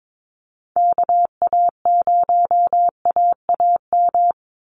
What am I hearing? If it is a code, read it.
Morse code, heard as a keyed tone, about 22 wpm, 700 Hz. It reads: KA0AAM
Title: Call Signs Repeated 2x - 22wpm